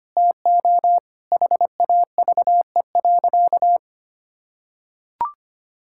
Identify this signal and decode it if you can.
Morse code, heard as a keyed tone, about 25 wpm, 700 Hz. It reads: TO HAVE.
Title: Sentences from Top 1000 Words - Review Part 5 - 25wpm